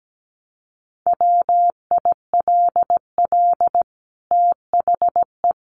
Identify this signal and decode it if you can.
Morse code, heard as a keyed tone, about 17 wpm, 700 Hz. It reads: WILL THE
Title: Sentences from Top 1500 Words 17wpm